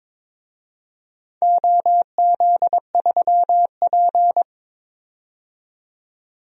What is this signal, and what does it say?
Morse code, heard as a keyed tone, about 22 wpm, 700 Hz. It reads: OZ3P
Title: DXCC-100 Call Signs - 4 Characters - 22wpm